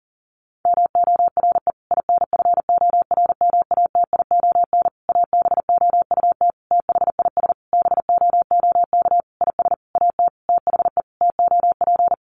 Morse code, heard as a keyed tone, about 40 wpm, 700 Hz. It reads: MORE INFORMATION ABOUT THIS BOOK IS AT THE TOP
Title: Sentences from Top 1200 Words 40wpm